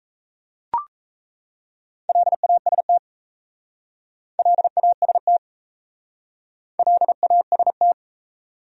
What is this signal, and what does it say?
Morse code, heard as a keyed tone, about 34 wpm, 700 Hz. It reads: LAST LAST LAST
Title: Speed-Racing Top 500 Words 28wpm